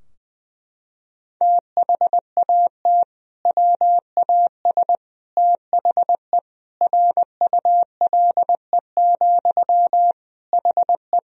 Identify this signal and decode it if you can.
Morse code, heard as a keyed tone, about 20 wpm, 700 Hz. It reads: THAT WAS THE RULE, HE
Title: Sentences from Top 1000 Words - Review Part 2 - 20wpm